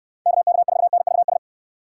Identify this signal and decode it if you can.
Morse code, heard as a keyed tone, about 66 wpm, 700 Hz. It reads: KK4NVS